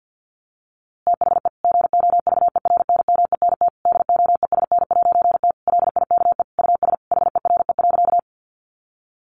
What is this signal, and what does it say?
Morse code, heard as a keyed tone, about 50 wpm, 700 Hz. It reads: THE GOVERNMENT DOESN'T LIKE US HERE.
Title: A Sound Of Thunder 50wpm